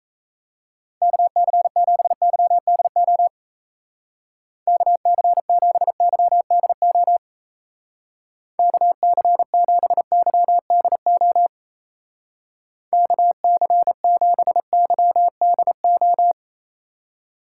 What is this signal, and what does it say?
Morse code, heard as a keyed tone, about 35 wpm, 700 Hz. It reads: KC7YDO KC7YDO KC7YDO KC7YDO
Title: Speed Racing - USA 2x3 Call Signs 28wpm